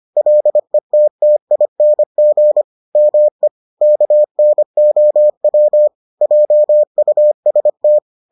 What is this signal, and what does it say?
Morse code, heard as a keyed tone, about 25 wpm, 600 Hz. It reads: LETTING ME KNOW JUST